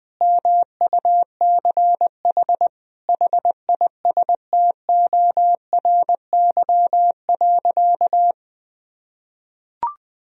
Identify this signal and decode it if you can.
Morse code, heard as a keyed tone, about 20 wpm, 700 Hz. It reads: MUCH HISTORY.
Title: Sentences from Top 1000 Words - Review Part 2 - 20wpm